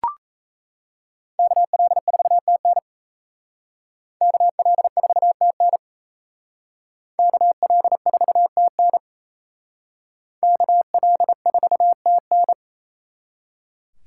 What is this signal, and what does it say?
Morse code, heard as a keyed tone, about 35 wpm, 700 Hz. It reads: KL4TN KL4TN KL4TN KL4TN